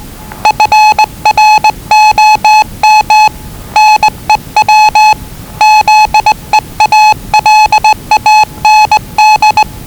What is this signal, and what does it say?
Morse code, heard as a keyed tone, about 20 wpm, 860 Hz. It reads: FROM NEW ZEALAND
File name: WellingtonRadioZlwCloseDownOnHfAndMf / 01-Zlw 4277khz 417.5 qrt until 0925z